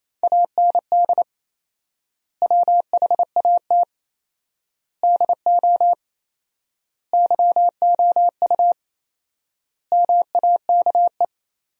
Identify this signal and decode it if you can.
Morse code, heard as a keyed tone, about 28 wpm, 700 Hz. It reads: AND WHAT DO YOU MAKE
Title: 4x Word-Spacing - Sentences from Top 100 Words - 28wpm